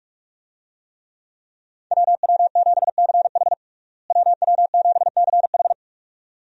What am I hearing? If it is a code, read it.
Morse code, heard as a keyed tone, about 42 wpm, 700 Hz. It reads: WW7CH WW7CH
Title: USA Call Signs - 5 Characters - Repeated 2x - 45wpm